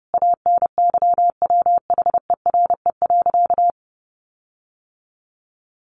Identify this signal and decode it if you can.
Morse code, heard as a keyed tone, about 28 wpm, 700 Hz. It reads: ANYWHERE.
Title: Sentences from Top 1400 Words 30wpm